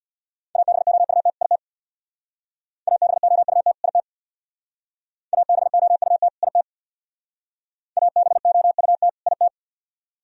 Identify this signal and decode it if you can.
Morse code, heard as a keyed tone, about 64 wpm, 700 Hz. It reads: ABOUT IT ABOUT IT ABOUT IT ABOUT IT